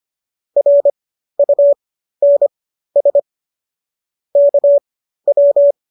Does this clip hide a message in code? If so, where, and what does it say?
Yes: Morse code from 0.6 to 5.7 s, heard as a keyed tone, about 25 wpm, spaced as for 13 wpm, 570 Hz. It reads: RUNS KW